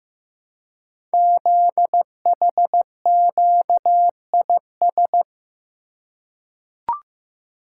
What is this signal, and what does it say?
Morse code, heard as a keyed tone, about 15 wpm, 700 Hz. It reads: ZHQIS
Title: Code Groups - 5 Letters - 15wpm